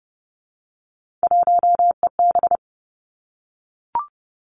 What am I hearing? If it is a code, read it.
Morse code, heard as a keyed tone, about 30 wpm, 700 Hz. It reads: 1EB
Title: Three Challenging Letter-Number Combinations 30wpm